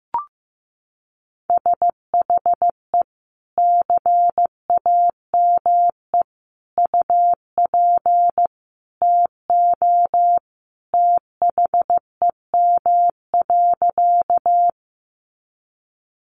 Morse code, heard as a keyed tone, about 15 wpm, 700 Hz. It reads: SHE CAME UP TO THEM.
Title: Sentences from Top 200 Words 15wpm